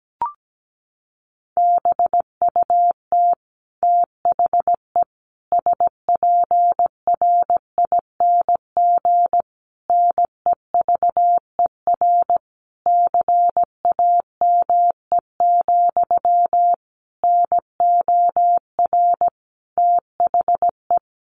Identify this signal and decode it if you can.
Morse code, heard as a keyed tone, about 17 wpm, 700 Hz. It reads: BUT THE SPRING NEVER CAME, NOR THE
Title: Sentences from Top 1500 Words 17wpm